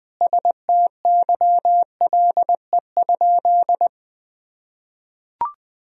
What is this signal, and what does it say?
Morse code, heard as a keyed tone, about 20 wpm, 700 Hz. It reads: STYLE?
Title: Sentences from Top 1000 Words - Review Part 4 - 20wpm